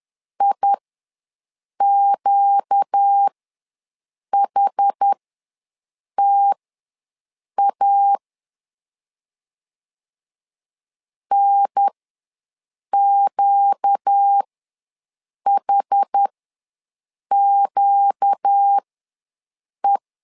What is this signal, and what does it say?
Morse code, heard as a keyed tone, about 10 wpm, spaced as for 6 wpm, 800 Hz. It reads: IQHTA NQHQE
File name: LEZIONI / 4 Q ws 10wpm 2306